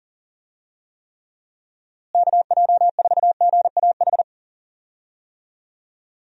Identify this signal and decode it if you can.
Morse code, heard as a keyed tone, about 37 wpm, 700 Hz. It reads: KJ4GAH
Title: USA Call Signs - 6 Characters - Repeated 3x - 40wpm